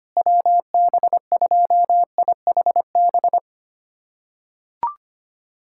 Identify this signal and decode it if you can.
Morse code, heard as a keyed tone, about 25 wpm, 700 Hz. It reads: WB2IHB
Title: USA Callsign DE Callsign 25wpm